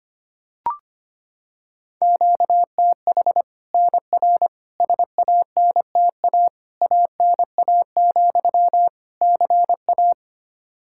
Morse code, heard as a keyed tone, about 25 wpm, 700 Hz. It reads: QTH NR SANTA ANA, CA
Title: Practice QSO - Exchange 1 - Beginning - 25wpm